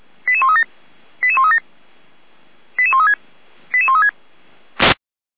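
Five-tone selective call (ZVEI), heard as quick runs of five tones, tones of about 70 ms. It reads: 88137 88137 88136 88136
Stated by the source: ZVEI five-tone call received by SDR, fire brigade feuerwehr-orlamuende.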